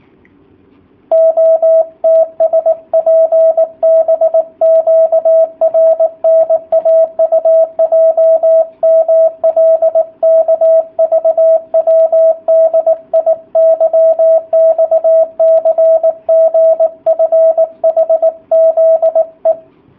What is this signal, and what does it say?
Morse code, heard as a keyed tone, about 18 wpm, 650 Hz, with radio noise. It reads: OTSPBQRNAUJMLKVWDIYXCGFHZE
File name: jh8cbh / 2019cw / sp18- w